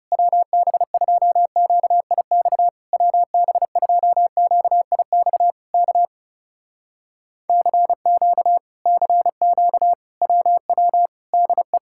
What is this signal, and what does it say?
Morse code, heard as a keyed tone, about 35 wpm, 700 Hz. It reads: WB2QIX WB2QIX K CQ CQ WW DE
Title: Speed-Racing Common QSO Phrases - 30wpm